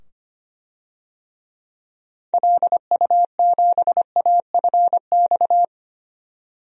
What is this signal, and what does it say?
Morse code, heard as a keyed tone, about 25 wpm, 700 Hz. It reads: LU7AFX
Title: DXCC-100 Call Signs - 6 Characters - 25wpm